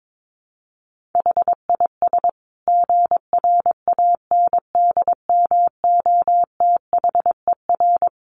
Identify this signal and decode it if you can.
Morse code, heard as a keyed tone, about 22 wpm, 700 Hz. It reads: HIS GRANDMOTHER
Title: Sentences from Top 1200 Words 22wpm